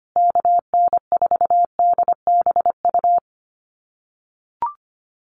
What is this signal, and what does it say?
Morse code, heard as a keyed tone, about 25 wpm, 700 Hz. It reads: KN4DBU
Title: USA Call Signs Repeated 3x 25wpm